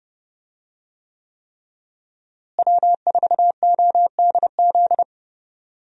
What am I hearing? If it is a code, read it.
Morse code, heard as a keyed tone, about 30 wpm, 700 Hz. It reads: W4ODZ
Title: USA Call Signs - 5 Characters - Repeated 3x - 30wpm